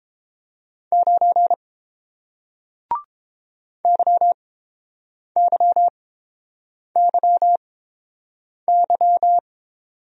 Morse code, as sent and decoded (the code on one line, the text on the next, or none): ----. / -.-- -.-- -.-- -.--
9 YYYY